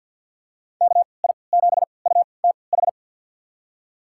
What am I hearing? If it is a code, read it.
Morse code, heard as a keyed tone, about 50 wpm, spaced as for 28 wpm, 700 Hz. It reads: KI7UTH
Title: USA Call Signs - 6 Characters - 3x Character Spacing - 50wpm